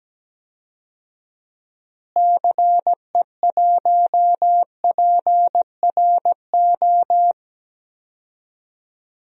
Morse code, heard as a keyed tone, about 17 wpm, 700 Hz. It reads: CE1PRO
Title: DXCC-100 Call Signs - 6 Characters - Repeated 2x - 17wpm